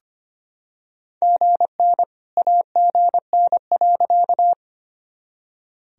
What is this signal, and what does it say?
Morse code, heard as a keyed tone, about 25 wpm, 700 Hz. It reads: GN AGN.
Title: Sentences from Top 300 Words and Abbreviations 25wpm